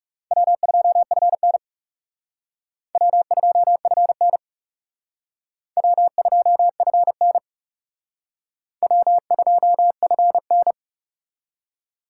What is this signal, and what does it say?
Morse code, heard as a keyed tone, about 37 wpm, 700 Hz. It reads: W2FN W2FN W2FN W2FN
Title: Speed Racing - USA Call Signs 30wpm